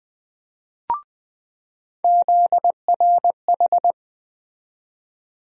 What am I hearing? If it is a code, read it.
Morse code, heard as a keyed tone, about 21 wpm, 700 Hz. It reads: ZRH
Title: USA Call Sign Postfixes - 3 Letters 20wpm